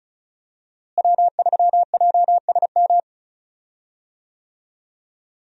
Morse code, heard as a keyed tone, about 36 wpm, 700 Hz. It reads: W3JSM
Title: USA Call Signs - 5 Characters - Repeated 3x - 35wpm